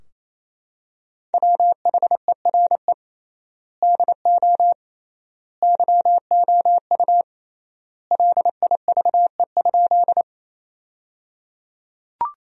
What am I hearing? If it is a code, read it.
Morse code, heard as a keyed tone, about 28 wpm, 700 Hz. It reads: WHERE DO YOU LIVE?
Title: 3x Word-Spacing - Sentences from Top 200 Words - 28wpm